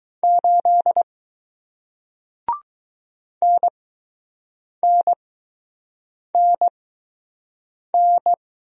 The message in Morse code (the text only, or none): 8 NNNN